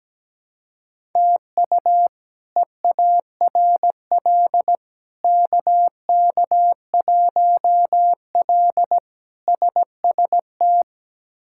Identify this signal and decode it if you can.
Morse code, heard as a keyed tone, about 17 wpm, 700 Hz. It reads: TU EARL KK1L SST
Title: K1USN SST QSO Practice - 17wpm